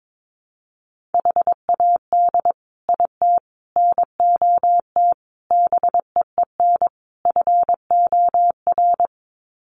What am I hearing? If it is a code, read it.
Morse code, heard as a keyed tone, about 22 wpm, 700 Hz. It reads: HAD IT NOT BEEN FOR
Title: Sets of 5 Words Encore2 22wpm